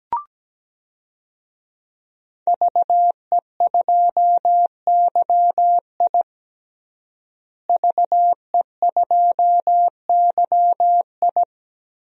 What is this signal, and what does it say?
Morse code, heard as a keyed tone, about 17 wpm, 700 Hz. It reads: VE2YI VE2YI
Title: DXCC-100 Call Signs - 5 Characters - Repeated 3x - 17wpm